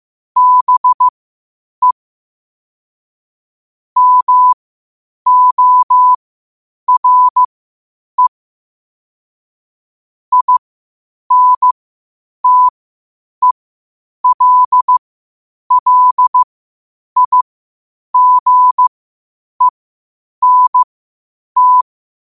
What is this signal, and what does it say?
Morse code, heard as a keyed tone, about 15 wpm, spaced as for 8 wpm, 1000 Hz. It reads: BE MORE INTELLIGENT
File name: qotdfiles / 8wpm